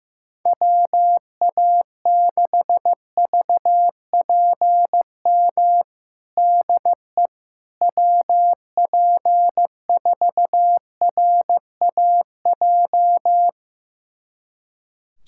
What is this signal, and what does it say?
Morse code, heard as a keyed tone, about 15 wpm, 700 Hz. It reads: WA6VPM DE WP4RAJ
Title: USA Callsign DE Callsign - 6 Characters - 15wpm